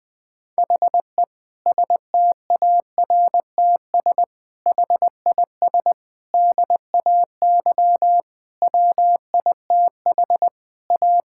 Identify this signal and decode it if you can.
Morse code, heard as a keyed tone, about 20 wpm, 700 Hz. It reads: HE STARTS HIS DAY WITH A
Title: Difficult Sentences with Hints - 20wpm